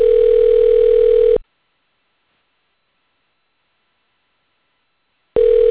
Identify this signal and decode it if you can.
Ringback tone (North America), 440 + 480 Hz.